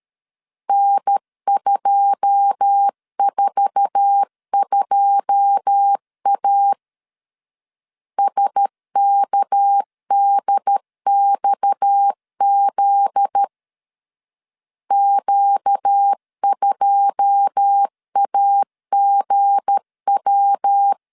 Morse code, heard as a keyed tone, about 13 wpm, 800 Hz. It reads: N242A SKDXZ Q2AGW
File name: LEZIONI / 9 2 12wpm 2301